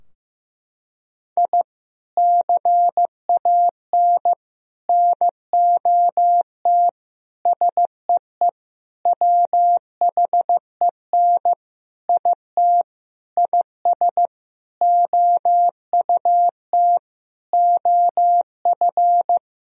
Morse code, heard as a keyed tone, about 16 wpm, 700 Hz. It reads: I CAN NOT SEE WHEN IT IS OUT OF
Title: Sentences from Top 1000 Words - Review Part 4 - 15wpm